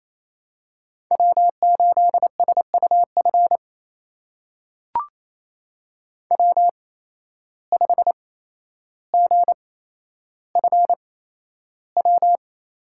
Morse code, heard as a keyed tone, about 28 wpm, spaced as for 12 wpm, 700 Hz. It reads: W8SUF W5GFW
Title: USA Call Signs - 5 Characters - 8x Character Spacing - 28wpm